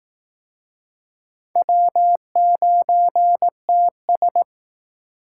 Morse code, heard as a keyed tone, about 18 wpm, 700 Hz. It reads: W9TS
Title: Speed Racing - USA Call Signs 15wpm